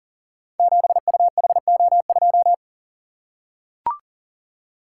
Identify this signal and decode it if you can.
Morse code, heard as a keyed tone, about 40 wpm, 690 Hz. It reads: 7UHO2
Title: Code Groups - 5 Letter:Numbers - 40wpm